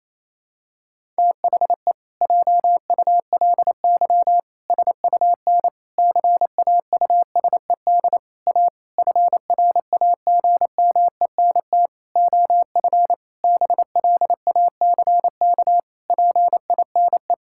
Morse code, heard as a keyed tone, about 28 wpm, 700 Hz. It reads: THE JULY SUN CAUSED A FRAGMENT OF BLACK PINE
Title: Pangram Challenge v2 - 28wpm